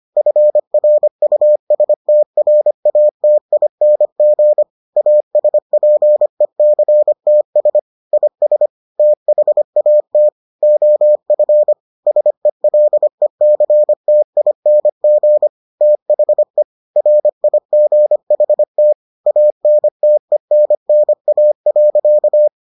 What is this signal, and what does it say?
Morse code, heard as a keyed tone, about 25 wpm, 600 Hz. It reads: FRUSTRATING ASPECTS IS THAT OF SELECTING THE RIGHT ANTENNA.